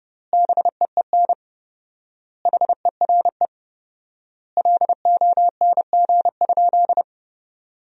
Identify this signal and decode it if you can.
Morse code, heard as a keyed tone, about 30 wpm, 700 Hz. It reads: BEEN HERE LONG?
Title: 4x Word-Spacing - Sentences from Top 200 Words - 30wpm